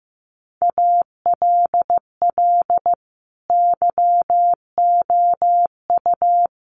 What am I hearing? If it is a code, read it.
Morse code, heard as a keyed tone, about 15 wpm, 700 Hz. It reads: ALL YOU
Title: Sets of 5 Words 15wpm